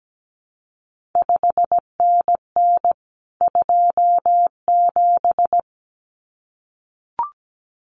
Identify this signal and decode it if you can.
Morse code, heard as a keyed tone, about 17 wpm, 700 Hz. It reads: 5NN 27
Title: CQ WW DX Contest Exchange - 17wpm